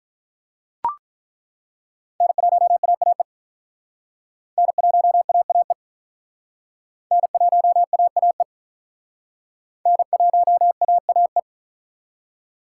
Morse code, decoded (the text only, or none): N1AAE N1AAE N1AAE N1AAE